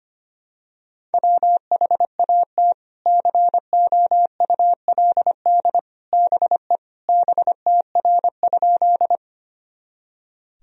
Morse code, heard as a keyed tone, about 25 wpm, 700 Hz. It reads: WHAT COULD BE BTR?
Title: Sentences from Top 400 Words and Abbreviations 25wpm